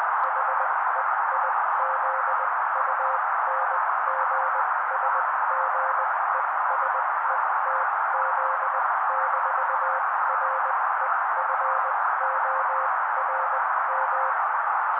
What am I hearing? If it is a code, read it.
Morse code, heard as a keyed tone, about 20 wpm, spaced as for 16 wpm, 530 Hz. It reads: HEIZUNGSGESETZ-REFORM
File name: daily / 16wpm noise